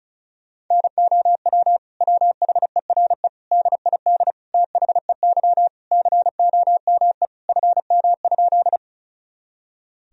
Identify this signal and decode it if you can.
Morse code, heard as a keyed tone, about 34 wpm, 700 Hz. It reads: NOW WHERE DID THEY COME FM?
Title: Sentences from Top 200 Words and Abbreviations 35wpm